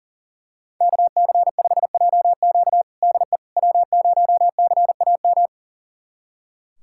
Morse code, heard as a keyed tone, about 40 wpm, 700 Hz. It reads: KC5JQ DE W0CAM
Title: USA Callsign DE Callsign - 5 Characters - 40wpm